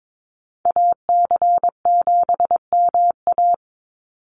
Morse code, heard as a keyed tone, about 22 wpm, 700 Hz. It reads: AC7MA
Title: NA SOTA Activator Callsigns 22wpm